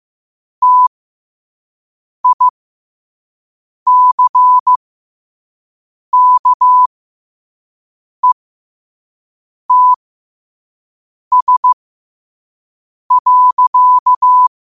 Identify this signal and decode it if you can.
Morse code, heard as a keyed tone, about 15 wpm, spaced as for 5 wpm, 1000 Hz. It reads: TICKETS.
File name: qotdfiles / 5wpm